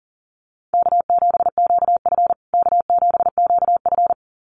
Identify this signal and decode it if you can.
Morse code, heard as a keyed tone, about 40 wpm, 700 Hz. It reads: K7QF K7QF
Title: USA Call Signs Repeated 2x 40wpm